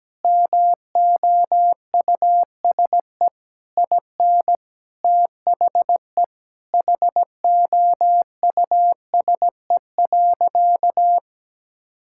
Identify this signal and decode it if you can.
Morse code, heard as a keyed tone, about 17 wpm, 700 Hz. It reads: MOUSE IN THE HOUSE.
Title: Sentences from Top 1000 Words - Review Part 5 - 17wpm